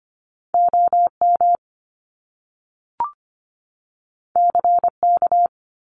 Morse code, heard as a keyed tone, about 25 wpm, 700 Hz. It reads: OM CK
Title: Top Two Letter Combinations in Words 25wpm